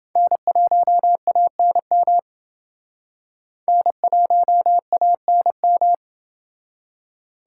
Morse code, heard as a keyed tone, about 29 wpm, 700 Hz. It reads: N1ANM N1ANM